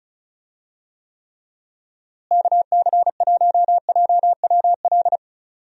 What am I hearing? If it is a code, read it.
Morse code, heard as a keyed tone, about 34 wpm, 700 Hz. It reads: KC1JWL
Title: USA Call Signs - 6 Characters - Repeated 2x - 35wpm